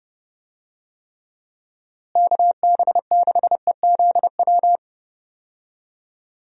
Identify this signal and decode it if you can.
Morse code, heard as a keyed tone, about 30 wpm, 700 Hz. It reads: KB6EZW
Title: USA Call Signs - 6 Characters - Repeated 2x - 30wpm